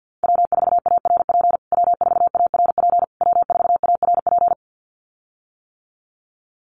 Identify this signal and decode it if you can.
Morse code, heard as a keyed tone, about 50 wpm, 700 Hz. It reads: W4ARP W4ARP W4ARP
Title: USA Call Signs Repeated 3x 50wpm